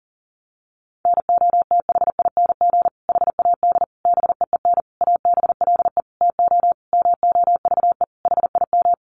Morse code, heard as a keyed tone, about 40 wpm, 700 Hz. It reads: NOTHING HAD BEEN ABLE TO MOVE HIM
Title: Sentences from Top 1300 Words 40wpm